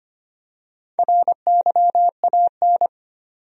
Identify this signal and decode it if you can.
Morse code, heard as a keyed tone, about 25 wpm, 700 Hz. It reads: RYAN
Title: USA Common Names - Max 4 Letters - 25wpm